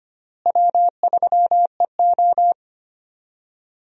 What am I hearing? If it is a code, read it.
Morse code, heard as a keyed tone, about 25 wpm, 700 Hz. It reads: W3EO
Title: USA Call Signs - 4 Characters - Repeated 3x - 25wpm